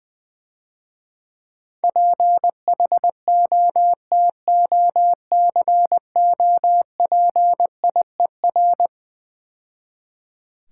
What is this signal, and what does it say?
Morse code, heard as a keyed tone, about 20 wpm, 700 Hz. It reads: PHOTOCOPIER
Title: Easy Words - Speed-Racing with Hints - 20wpm